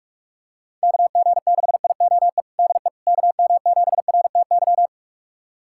Morse code, heard as a keyed tone, about 46 wpm, 690 Hz. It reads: KG6IOE DE KM7RTY
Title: USA Callsign DE Callsign 45wpm